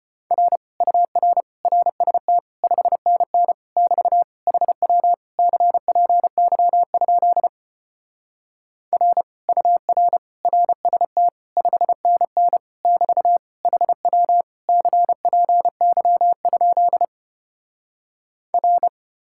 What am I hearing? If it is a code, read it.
Morse code, heard as a keyed tone, about 33 wpm, 700 Hz. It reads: R UR RST 5NN <BT> HW CPY? R UR RST 5NN <BT> HW CPY? R